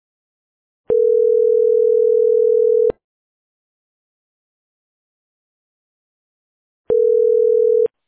Ringback tone (North America), 440 + 480 Hz.